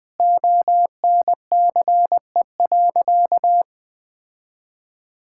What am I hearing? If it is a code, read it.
Morse code, heard as a keyed tone, about 21 wpm, 700 Hz. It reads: ONCE.
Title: Sentences from Top 1000 Words - Review Part 5 - 20wpm